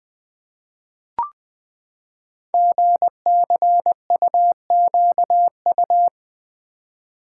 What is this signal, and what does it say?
Morse code, heard as a keyed tone, about 20 wpm, 700 Hz. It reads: GCUQU
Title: Code Groups - 5 Letters - 20wpm